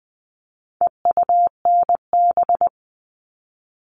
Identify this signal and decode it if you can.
Morse code, heard as a keyed tone, about 20 wpm, 700 Hz. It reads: EUNB
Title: All Cut Numbers - 4-Digit 20wpm